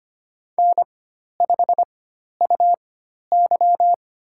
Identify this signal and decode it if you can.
Morse code, heard as a keyed tone, about 25 wpm, spaced as for 12 wpm, 700 Hz. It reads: N5UY
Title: USA Call Signs - 4x Character Spacing - 25wpm